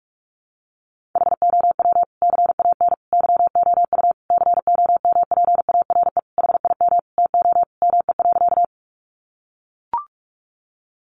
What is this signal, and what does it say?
Morse code, heard as a keyed tone, about 44 wpm, 700 Hz. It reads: HOW CAN YOU COMPARE HIM TO ME.
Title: Sentences from Top 1300 Words 45wpm